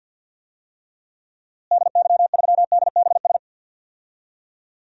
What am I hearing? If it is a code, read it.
Morse code, heard as a keyed tone, about 51 wpm, 700 Hz. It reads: DY3DBS
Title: DXCC-100 Call Signs - 6 Characters - Repeated 2x - 50wpm